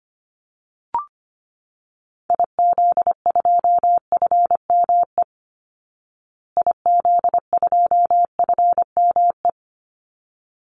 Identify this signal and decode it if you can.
Morse code, heard as a keyed tone, about 25 wpm, 700 Hz. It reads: IZ2FME IZ2FME
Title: Call Signs Repeated 3x - 25wpm